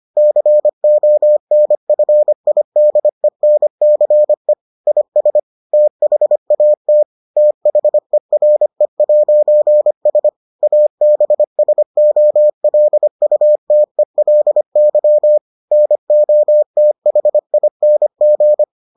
Morse code, heard as a keyed tone, about 25 wpm, 600 Hz. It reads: CONFIDENCE IS THAT THERE'S ABSOLUTELY NOTHING